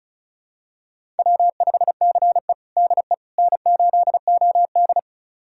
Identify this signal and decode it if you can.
Morse code, heard as a keyed tone, about 34 wpm, 700 Hz. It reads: W5CE DE N8OD
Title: USA Callsign DE Callsign - 4 Characters - 35wpm